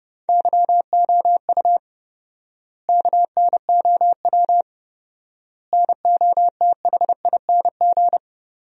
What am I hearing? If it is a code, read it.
Morse code, heard as a keyed tone, about 32 wpm, 700 Hz. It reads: YOU KNOW NOTHING
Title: 4x Word-Spacing - Sentences from Top 500 Words - 30wpm